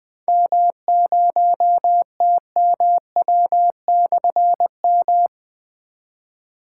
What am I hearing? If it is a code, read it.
Morse code, heard as a keyed tone, about 20 wpm, 700 Hz. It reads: M0TMW/M
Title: Difficult Call Signs 20wpm